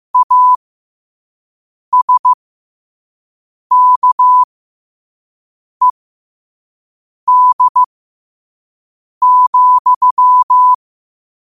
Morse code, heard as a keyed tone, about 15 wpm, spaced as for 5 wpm, 1000 Hz. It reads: ASKED,
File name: qotdfiles / 5wpm